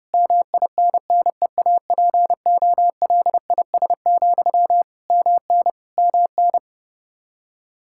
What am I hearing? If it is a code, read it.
Morse code, heard as a keyed tone, about 30 wpm, 700 Hz. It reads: MINNEAPOLIS, MN MN
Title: Practice QSO - Exchange 1 - Beginning - 30wpm